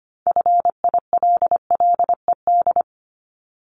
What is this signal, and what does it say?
Morse code, heard as a keyed tone, about 25 wpm, 700 Hz. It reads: FILLED
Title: Multi-letter Phonemes 25wpm